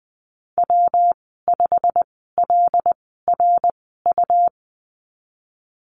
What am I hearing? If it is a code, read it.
Morse code, heard as a keyed tone, about 20 wpm, spaced as for 14 wpm, 700 Hz. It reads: W5LRU
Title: 154 Lesson 25g - Letter p - Call Signs 20wpm